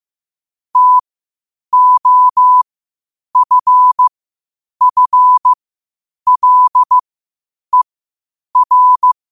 Morse code, heard as a keyed tone, about 15 wpm, spaced as for 8 wpm, 1000 Hz. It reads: TOFFLER
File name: qotdfiles / 8wpm